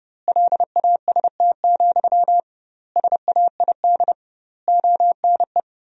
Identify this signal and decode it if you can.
Morse code, heard as a keyed tone, about 30 wpm, 700 Hz. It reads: LAST, SAID ONE
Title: 2x Word-Spacing - Sentences from Top 300 Words - 30wpm